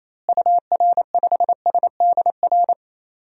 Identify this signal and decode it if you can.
Morse code, heard as a keyed tone, about 28 wpm, 700 Hz. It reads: UR5SDR
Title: DXCC-100 Call Signs - 6 Characters - Repeated 3x - 28wpm